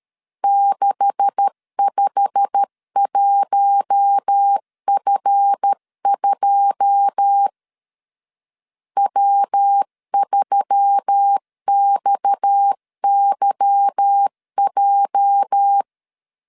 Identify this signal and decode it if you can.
Morse code, heard as a keyed tone, about 13 wpm, 800 Hz. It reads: 651F2 W3XYJ
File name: LEZIONI / 9 286 12wpm 2305